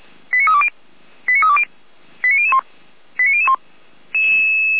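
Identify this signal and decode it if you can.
Five-tone selective call (ZVEI), heard as quick runs of five tones, tones of about 70 ms. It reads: 89320 89320 89001 89001
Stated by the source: Five-tone ZVEI selective call received on an SDR.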